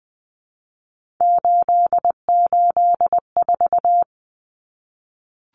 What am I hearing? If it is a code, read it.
Morse code, heard as a keyed tone, about 20 wpm, 700 Hz. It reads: 884
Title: Common Cut Numbers - 3-Digit 20wpm